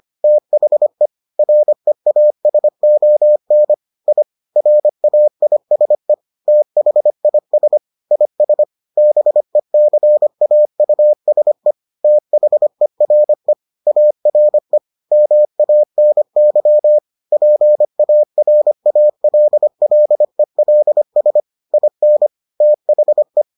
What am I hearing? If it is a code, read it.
Morse code, heard as a keyed tone, about 25 wpm, 600 Hz. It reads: THE REASON I RAISE THIS IS BECAUSE THERE ARE MANY PARALLELS IN THE